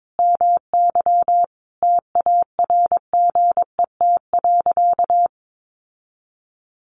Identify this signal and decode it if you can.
Morse code, heard as a keyed tone, about 22 wpm, 700 Hz. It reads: MY TARGET.
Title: Sentences from Top 1200 Words 22wpm